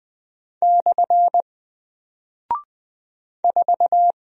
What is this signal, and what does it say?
Morse code, heard as a keyed tone, about 20 wpm, 700 Hz. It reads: / 4
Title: Single Character (Letter:Number:Punctuation:Prosign) Alt 20wpm